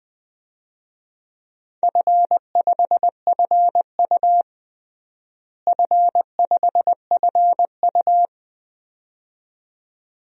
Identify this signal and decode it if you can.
Morse code, heard as a keyed tone, about 20 wpm, 700 Hz. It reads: F5FU F5FU
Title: DXCC-100 Call Signs - 4 Characters - Repeated 2x - 20wpm